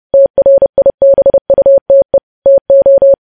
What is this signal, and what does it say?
Morse code, heard as a keyed tone, about 30 wpm, 550 Hz. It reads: TRIBUTE TO
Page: news headlines in morse code at 30 wpm